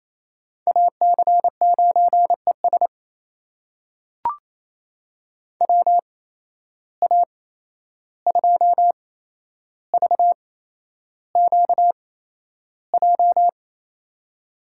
Morse code, heard as a keyed tone, about 27 wpm, spaced as for 11 wpm, 700 Hz. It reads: AC9ES WA2VQJ
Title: USA Call Signs - 8x Character Spacing - 28wpm